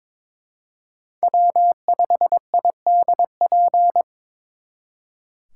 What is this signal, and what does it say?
Morse code, heard as a keyed tone, about 26 wpm, 700 Hz. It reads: W5IDP
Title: Speed Racing - USA Call Signs 22wpm